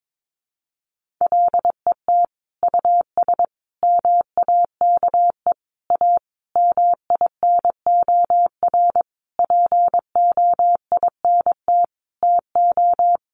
Morse code, heard as a keyed tone, about 22 wpm, 700 Hz. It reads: LET US MAKE A MINOR POINT TO
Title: Sentences from Top 1500 Words 22wpm